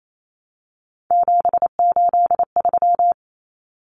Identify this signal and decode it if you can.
Morse code, heard as a keyed tone, about 28 wpm, 700 Hz. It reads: 783
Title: Common Cut Numbers - 3-Digit 28wpm